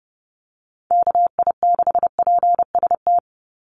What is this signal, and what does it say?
Morse code, heard as a keyed tone, about 30 wpm, 700 Hz. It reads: KI6PST
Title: USA Call Signs 30wpm